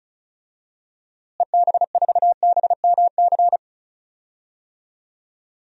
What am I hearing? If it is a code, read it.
Morse code, heard as a keyed tone, about 36 wpm, 700 Hz. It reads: EB4BMC
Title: DXCC-100 Call Signs - 6 Characters - Repeated 3x - 35wpm